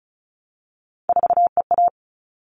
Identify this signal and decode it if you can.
Morse code, heard as a keyed tone, about 34 wpm, 700 Hz. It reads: 4EA
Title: All Cut Numbers - 3-Digit 35wpm